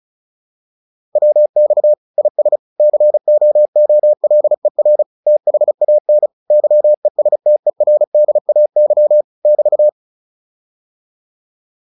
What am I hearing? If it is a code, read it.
Morse code, heard as a keyed tone, about 35 wpm, 600 Hz. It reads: WX IS COOLER THAN YESTERDAY <BT>
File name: QSO409 35